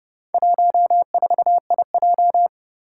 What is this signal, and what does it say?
Morse code, heard as a keyed tone, about 30 wpm, 700 Hz. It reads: 14IJ
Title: Four Challenging Letter-Number Combinations 30wpm